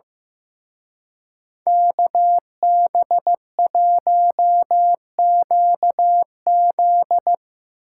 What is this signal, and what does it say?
Morse code, heard as a keyed tone, about 17 wpm, 700 Hz. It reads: KB1QZ
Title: Speed Racing - USA Call Signs 15wpm